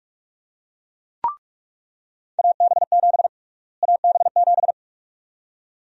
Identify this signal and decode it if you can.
Morse code, heard as a keyed tone, about 46 wpm, 700 Hz. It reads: AB7 AB7
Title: USA Call Sign Prefixes - Repeated 2x - 45wpm